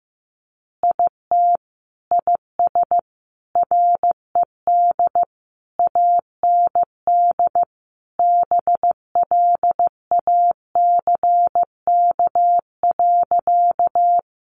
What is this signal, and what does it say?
Morse code, heard as a keyed tone, about 15 wpm, 700 Hz. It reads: IT IS RED AND BLACK.
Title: Easy Phrases 15wpm